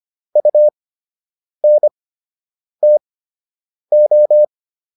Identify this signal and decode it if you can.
Morse code, heard as a keyed tone, about 25 wpm, spaced as for 8 wpm, 600 Hz. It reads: UNTO